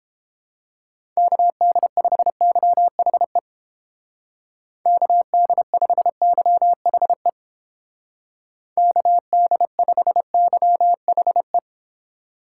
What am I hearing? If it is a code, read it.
Morse code, heard as a keyed tone, about 31 wpm, 700 Hz. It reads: KD5YHE KD5YHE KD5YHE